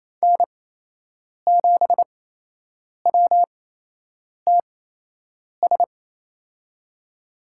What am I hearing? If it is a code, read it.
Morse code, heard as a keyed tone, about 28 wpm, spaced as for 8 wpm, 700 Hz. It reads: N7WTS
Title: USA Call Signs - 8x Character Spacing - 28wpm